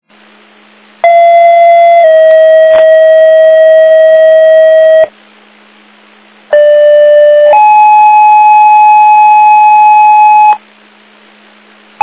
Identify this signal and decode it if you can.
Two-tone page, heard as a one-second tone, then a three-second tone, about 680 then 640 Hz, about 590 then 850 Hz. It is heard from 1.0 to 10.6 s.